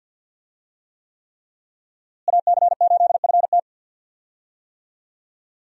Morse code, heard as a keyed tone, about 52 wpm, 700 Hz. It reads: AC8FT
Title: USA Call Signs - 5 Characters - Repeated 3x - 50wpm